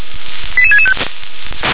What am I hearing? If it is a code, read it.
Five-tone selective call (ZVEI), heard as quick runs of five tones, tones of about 70 ms. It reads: 88665